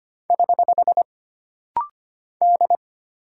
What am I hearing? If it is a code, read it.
Morse code, heard as a keyed tone, about 25 wpm, 700 Hz. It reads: <HH> D
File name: Single Character (Letter:Number:Punctuation:Prosign) - Rapid-Fire - 25wpm